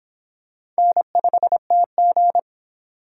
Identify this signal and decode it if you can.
Morse code, heard as a keyed tone, about 28 wpm, 700 Hz. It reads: N5TG